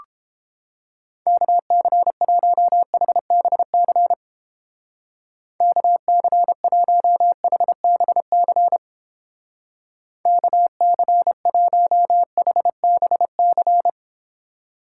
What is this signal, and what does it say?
Morse code, heard as a keyed tone, about 30 wpm, 700 Hz. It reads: KC1HBC KC1HBC KC1HBC